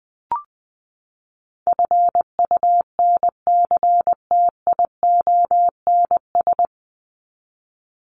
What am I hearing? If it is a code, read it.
Morse code, heard as a keyed tone, about 19 wpm, 700 Hz. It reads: FUNCTIONS
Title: ES:S Suffix - Top 100 Words with Base Words 20wpm